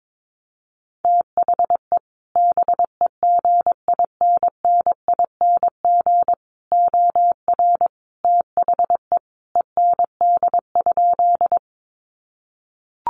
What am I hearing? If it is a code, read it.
Morse code, heard as a keyed tone, about 22 wpm, 700 Hz. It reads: THE BEGINNING OR THE END?
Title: Sentences from Top 800 Words 22wpm